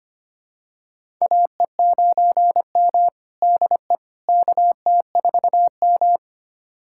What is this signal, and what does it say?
Morse code, heard as a keyed tone, about 25 wpm, 700 Hz. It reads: AE9M DE KT4M
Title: USA Callsign DE Callsign - 4 Characters - 25wpm